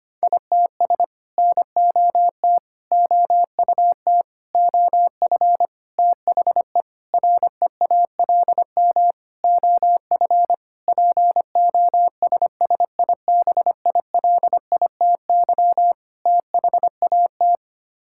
Morse code, heard as a keyed tone, about 25 wpm, 700 Hz. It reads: ITS NOT OUT OF THE REALM OF POSSIBILITY THAT
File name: Sets of 9 Words - 25wpm